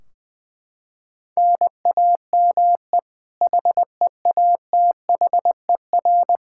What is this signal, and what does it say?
Morse code, heard as a keyed tone, about 21 wpm, 700 Hz. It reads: NAME HEATHER
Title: K1USN SST QSO Practice 20wpm